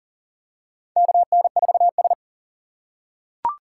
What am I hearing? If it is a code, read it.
Morse code, heard as a keyed tone, about 40 wpm, 700 Hz. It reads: KN4S
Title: USA Call Signs - 6x Character Spacing - 40wpm